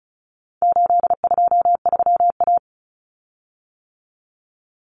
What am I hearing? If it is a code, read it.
Morse code, heard as a keyed tone, about 34 wpm, 700 Hz. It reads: 823A
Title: Common Cut Numbers - 4-Digit 35wpm